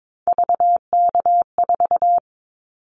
Morse code, heard as a keyed tone, about 22 wpm, 700 Hz. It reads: VK4
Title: Callsign Prefixes for Top 100 DXCC - 22wpm